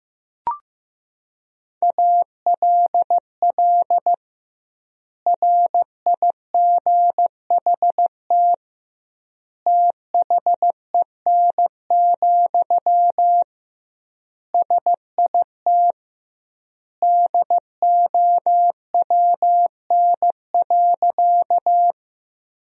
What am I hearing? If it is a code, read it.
Morse code, heard as a keyed tone, about 15 wpm, 700 Hz. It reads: ALL RIGHT THEN, SIT DOWN.
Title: 2x Word-Spacing - Sentences from Top 500 Words - 15wpm